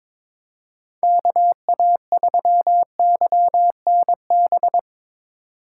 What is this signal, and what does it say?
Morse code, heard as a keyed tone, about 24 wpm, 700 Hz. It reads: KA3YNB
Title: Speed Racing - USA 2x3 Call Signs 22wpm